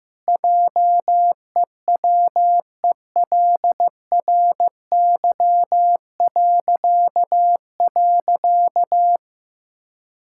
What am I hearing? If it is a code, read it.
Morse code, heard as a keyed tone, about 16 wpm, 700 Hz. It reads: JEWELRY..
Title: Pangram Challenge v2 - 15wpm